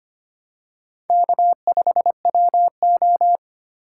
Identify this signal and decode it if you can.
Morse code, heard as a keyed tone, about 25 wpm, 700 Hz. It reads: K5WO
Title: USA Call Signs - 4 Characters - 6x Character Spacing - 25wpm